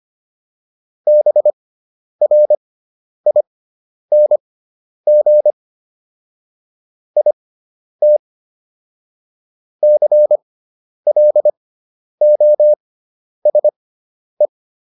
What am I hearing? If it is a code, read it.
Morse code, heard as a keyed tone, about 25 wpm, spaced as for 10 wpm, 600 Hz. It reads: BRING IT CLOSE